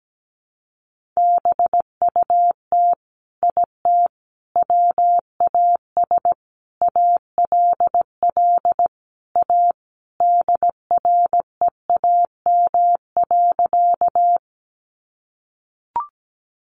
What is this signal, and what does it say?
Morse code, heard as a keyed tone, about 17 wpm, 700 Hz. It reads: BUT IT WAS ALL A DREAM.
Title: Sentences from Top 1500 Words 17wpm